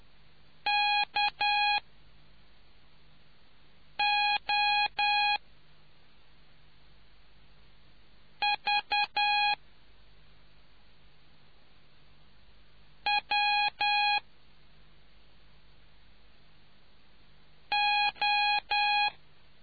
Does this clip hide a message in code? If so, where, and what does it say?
Yes: Morse code from 0.7 to 19.1 s, heard as a keyed tone, about 10 wpm, spaced as for 3 wpm, 830 Hz. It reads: KO V W O